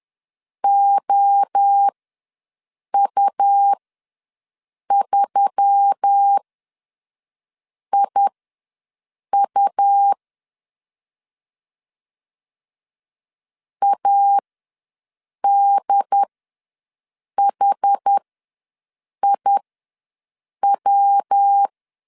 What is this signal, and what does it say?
Morse code, heard as a keyed tone, about 10 wpm, spaced as for 6 wpm, 800 Hz. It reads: OU3IU ADHIW
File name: LEZIONI / 5 LBVC / 5 L ws 10wpm 3636